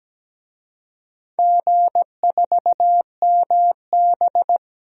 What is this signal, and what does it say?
Morse code, heard as a keyed tone, about 17 wpm, 700 Hz. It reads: G4MB
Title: DXCC-100 Call Signs - 4 Characters - Repeated 2x - 17wpm